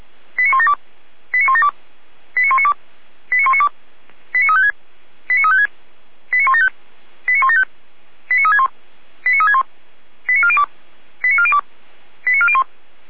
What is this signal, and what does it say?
Five-tone selective call (ZVEI), heard as quick runs of five tones, tones of about 70 ms. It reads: 89172 89172 89192 89192 89356 89356 89175 89175 89361 89361 89402 89402 89401